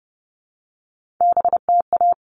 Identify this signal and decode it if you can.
Morse code, heard as a keyed tone, about 30 wpm, 700 Hz. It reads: BTA
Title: All Cut Numbers - 3-Digit 30wpm